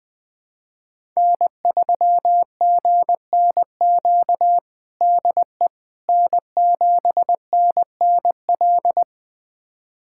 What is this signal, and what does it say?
Morse code, heard as a keyed tone, about 20 wpm, 700 Hz. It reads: N3GNQ DE N7NNL
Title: USA Callsign DE Callsign - 5 Characters - 20wpm